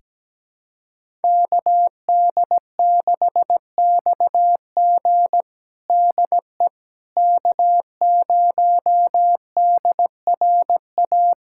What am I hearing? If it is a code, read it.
Morse code, heard as a keyed tone, about 17 wpm, 700 Hz. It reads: KD6XG DE K0DRA
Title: USA Callsign DE Callsign - 5 Characters - 17wpm